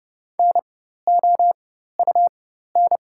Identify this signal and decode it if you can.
Morse code, heard as a keyed tone, about 32 wpm, spaced as for 14 wpm, 700 Hz. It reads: NOUN
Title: 4x Character Spacing - Top 500 Words - 30wpm